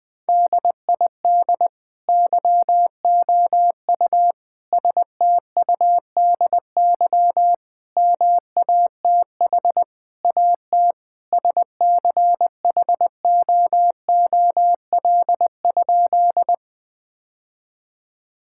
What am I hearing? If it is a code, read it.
Morse code, heard as a keyed tone, about 20 wpm, 700 Hz. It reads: DID YOU STUDY MATH AT SCHOOL?
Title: Sentences from Top 1000 Words - Review Part 6 - 20wpm